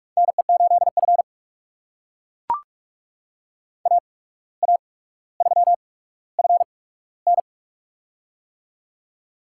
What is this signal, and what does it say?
Morse code, heard as a keyed tone, about 42 wpm, spaced as for 17 wpm, 700 Hz. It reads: NE8F AA3FN
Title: USA Call Signs - 8x Character Spacing - 45wpm